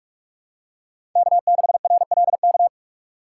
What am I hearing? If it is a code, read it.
Morse code, heard as a keyed tone, about 46 wpm, 700 Hz. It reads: K6RLK
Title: USA Call Signs - 2x Character Spacing - 45wpm